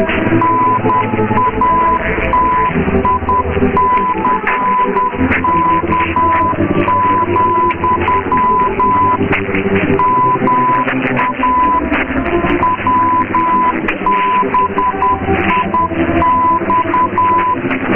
Morse code, heard as a keyed tone, about 10 wpm, 1000 Hz, with radio noise. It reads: NATICK, MA WBIX
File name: mw / 0650 approx wbix test